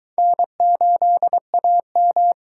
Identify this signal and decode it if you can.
Morse code, heard as a keyed tone, about 22 wpm, 700 Hz. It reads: N8AM